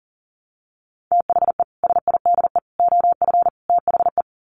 Morse code, heard as a keyed tone, about 40 wpm, 700 Hz. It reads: THE SIDE OF THE
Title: Sets of 4 Words 40wpm